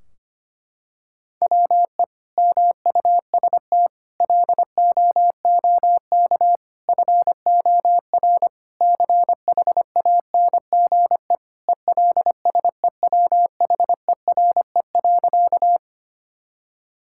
Morse code, heard as a keyed tone, about 25 wpm, 700 Hz. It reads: WE MUST LOOK FOR CHANGE ELSEWHERE.
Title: Sentences from Top 1600 Words 25wpm